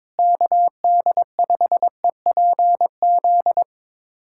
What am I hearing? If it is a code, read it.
Morse code, heard as a keyed tone, about 22 wpm, 700 Hz. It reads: KD5EPZ
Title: USA Callsign DE Callsign - 6 Characters - 22wpm